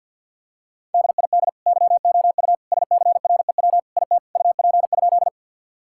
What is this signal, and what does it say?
Morse code, heard as a keyed tone, about 50 wpm, 700 Hz. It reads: DID YOU SCREW IT UP?
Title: Sentences from Top 1600 Words 50wpm